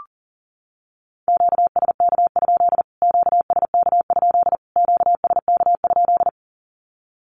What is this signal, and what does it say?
Morse code, heard as a keyed tone, about 37 wpm, 700 Hz. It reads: QSK? QSK? QSK?
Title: Q-Codes Repeated 3x 40wpm